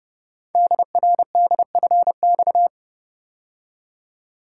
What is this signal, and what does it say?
Morse code, heard as a keyed tone, about 29 wpm, 700 Hz. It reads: DRDFX
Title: Code Groups - 5 Letters - 30wpm